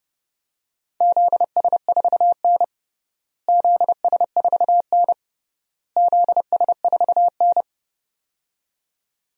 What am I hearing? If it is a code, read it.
Morse code, heard as a keyed tone, about 32 wpm, 700 Hz. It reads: ZS4N ZS4N ZS4N
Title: DXCC-100 Call Signs - 4 Characters - Repeated 3x - 30wpm